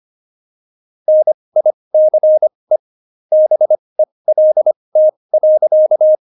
Morse code, heard as a keyed tone, about 25 wpm, spaced as for 20 wpm, 620 Hz. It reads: NICE BELT.
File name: Jokes-25-char-20-eff